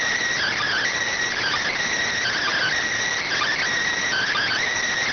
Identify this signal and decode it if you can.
Emergency Alert System (SAME) tones (header bursts), heard as harsh buzzing bursts. It is heard from 0.3 to 4.7 s.
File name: findings / CT AXIAL -572mm 2.gen-38 rx